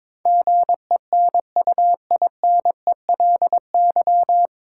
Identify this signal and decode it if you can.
Morse code, heard as a keyed tone, about 21 wpm, 700 Hz. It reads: GENUINELY
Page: Morse Code Ninja practice 22wpm